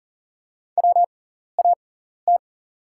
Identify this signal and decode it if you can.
Morse code, heard as a keyed tone, about 37 wpm, spaced as for 13 wpm, 700 Hz. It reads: WAT
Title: 6x Character Spacing - Top 300 Words - 40wpm